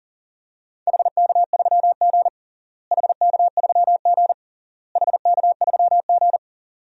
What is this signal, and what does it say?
Morse code, heard as a keyed tone, about 40 wpm, 700 Hz. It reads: HK3G HK3G HK3G
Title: DXCC-100 Call Signs - 4 Characters - Repeated 3x - 40wpm